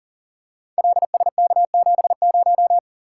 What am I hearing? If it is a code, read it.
Morse code, heard as a keyed tone, about 38 wpm, 690 Hz. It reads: LSK70
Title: Code Groups - 5 Letter:Numbers - 40wpm